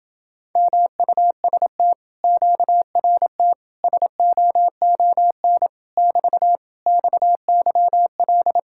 Morse code, heard as a keyed tone, about 28 wpm, 700 Hz. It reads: MUST QRT SOON <BT> XYL